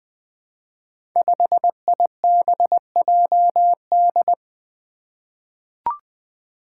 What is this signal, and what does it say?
Morse code, heard as a keyed tone, about 20 wpm, 700 Hz. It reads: 5IBJD
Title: Five Challenging Letter-Number Combinations 20wpm